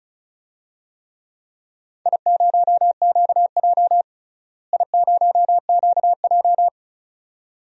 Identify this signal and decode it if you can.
Morse code, heard as a keyed tone, about 34 wpm, 700 Hz. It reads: I0QJ I0QJ
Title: DXCC-100 Call Signs - 4 Characters - Repeated 2x - 35wpm